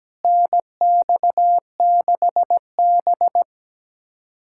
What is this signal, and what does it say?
Morse code, heard as a keyed tone, about 17 wpm, 700 Hz. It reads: NX6B
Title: USA Callsign DE Callsign - 4 Characters - 17wpm